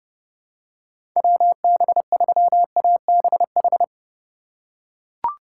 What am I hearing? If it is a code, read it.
Morse code, heard as a keyed tone, about 30 wpm, 700 Hz. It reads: WB3ABH
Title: USA Call Signs - 4x Character Spacing - 30wpm